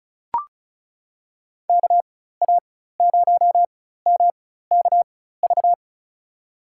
Morse code, heard as a keyed tone, about 36 wpm, spaced as for 16 wpm, 700 Hz. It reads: KA0MKV
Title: USA Call Signs - 6 Characters - 4x Character Spacing - 35wpm